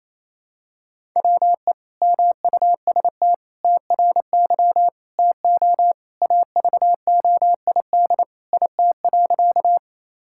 Morse code, heard as a keyed tone, about 28 wpm, 700 Hz. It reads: WE MUST TRY TO AVOID IT.
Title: Sentences from Top 1000 Words - Review Part 6 - 28wpm